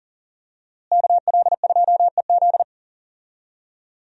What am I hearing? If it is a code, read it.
Morse code, heard as a keyed tone, about 40 wpm, 700 Hz. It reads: KL2EZ
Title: USA Call Signs - 5 Characters - Repeated 3x - 40wpm